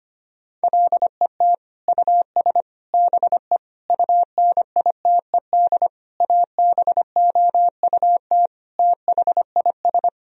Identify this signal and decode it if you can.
Morse code, heard as a keyed tone, about 25 wpm, 700 Hz. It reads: LET US BE UNITED ABOUT THIS
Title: Sentences from Top 1000 Words - Review Part 4 - 25wpm